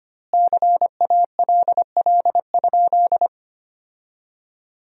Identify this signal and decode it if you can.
Morse code, heard as a keyed tone, about 25 wpm, 700 Hz. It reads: CALL?
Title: 2x Word-Spacing - Sentences from Top 100 Words - 25wpm